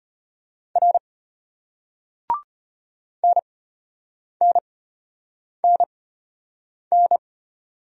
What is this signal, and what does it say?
Morse code, heard as a keyed tone, about 34 wpm, spaced as for 8 wpm, 700 Hz. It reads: R NNNN